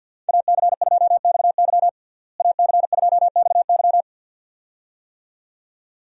Morse code, heard as a keyed tone, about 51 wpm, 700 Hz. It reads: AC2XY AC2XY
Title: USA Call Signs - 5 Characters - Repeated 2x - 50wpm